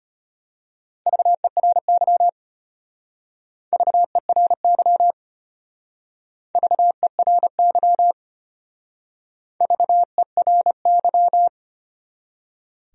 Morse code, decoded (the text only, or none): VERY VERY VERY VERY